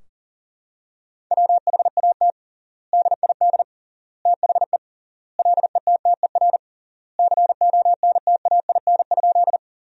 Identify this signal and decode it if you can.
Morse code, heard as a keyed tone, about 37 wpm, 700 Hz. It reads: WHAT DID THE LETTER CONTAIN?
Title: 3x Word-Spacing - Sentences from Top 500 Words - 40wpm